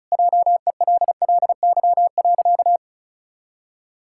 Morse code, heard as a keyed tone, about 36 wpm, 700 Hz. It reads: JELLY.
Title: Pangram Challenge v2 - 35wpm